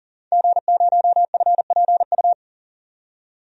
Morse code, heard as a keyed tone, about 40 wpm, 700 Hz. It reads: G0FPU
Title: DXCC-100 Call Signs - 5 Characters - 40wpm